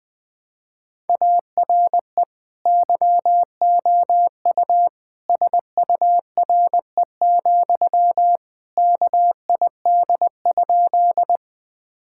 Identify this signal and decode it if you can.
Morse code, heard as a keyed tone, about 20 wpm, 700 Hz. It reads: ARE YOU SURE, KID?
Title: Sentences from Top 1600 Words 20wpm